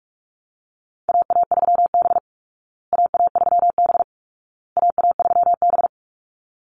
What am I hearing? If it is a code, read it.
Morse code, heard as a keyed tone, about 44 wpm, 700 Hz. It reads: AA3B AA3B AA3B
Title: Call Signs Repeated 2x - 45wpm